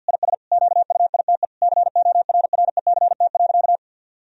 Morse code, heard as a keyed tone, about 52 wpm, 700 Hz. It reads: IS QUITE CORRECT.
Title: Sentences from Top 1000 Words - Review Part 2 - 50wpm